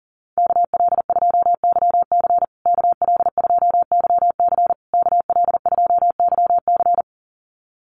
Morse code, heard as a keyed tone, about 40 wpm, 700 Hz. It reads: KL2YC KL2YC KL2YC
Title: USA Call Signs Repeated 3x 40wpm